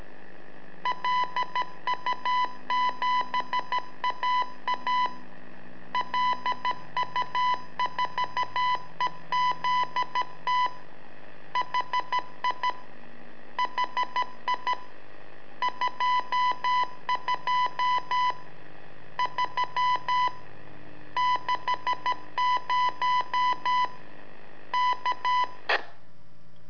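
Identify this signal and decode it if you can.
Morse code, heard as a keyed tone, about 13 wpm, 1000 Hz. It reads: LU7AA LU4EZT HI HI 22 3 60 K
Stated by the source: Telemetry in Morse from LU7AA and LU4EZ, AMSAT and Turma Mar del Plata, sent from a balloon.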